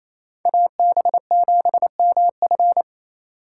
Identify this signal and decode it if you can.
Morse code, heard as a keyed tone, about 36 wpm, 700 Hz. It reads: AB7MF